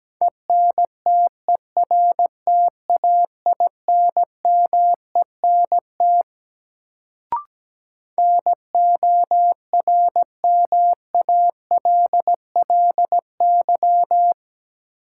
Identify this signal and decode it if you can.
Morse code, heard as a keyed tone, about 17 wpm, 700 Hz. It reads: ENTERTAINMENT NORMALLY
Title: Sentences from Top 1000 Words - Review Part 6 - New Word Drill - 17wpm